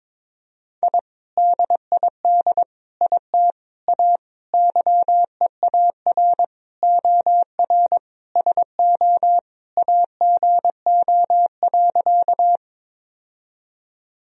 Morse code, heard as a keyed tone, about 22 wpm, 700 Hz. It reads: I DID IT A YEAR OR SO AGO.
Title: Sentences from Top 1000 Words - Review Part 3 - 22wpm